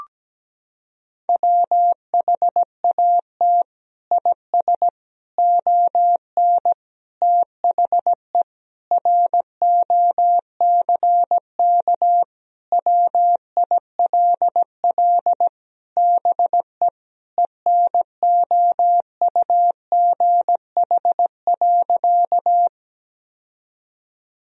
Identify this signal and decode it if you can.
Morse code, heard as a keyed tone, about 17 wpm, 700 Hz. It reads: WHAT IS ON THE ROCK WILL BE ENOUGH.
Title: Sentences from Top 1000 Words - Review Part 2 - 17wpm